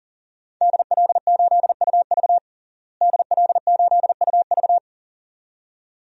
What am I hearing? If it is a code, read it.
Morse code, heard as a keyed tone, about 40 wpm, 700 Hz. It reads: DL8UV DL8UV
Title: DXCC-100 Call Signs - 5 Characters - Repeated 2x - 40wpm